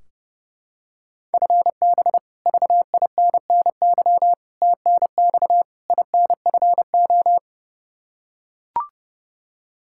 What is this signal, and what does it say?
Morse code, heard as a keyed tone, about 30 wpm, 700 Hz. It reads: FB VINNY TNX INFO
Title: Practice QSO - Exchange 3 - More Info - 30wpm